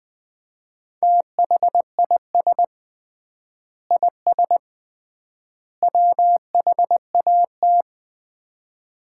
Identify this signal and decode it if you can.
Morse code, heard as a keyed tone, about 20 wpm, 700 Hz. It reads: THIS IS WHAT
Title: 3x Word-Spacing - Sets of 3 Words - 20wpm